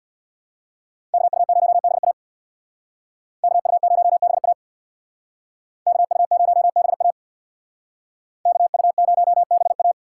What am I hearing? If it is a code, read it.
Morse code, heard as a keyed tone, about 65 wpm, 700 Hz. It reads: KU0BA KU0BA KU0BA KU0BA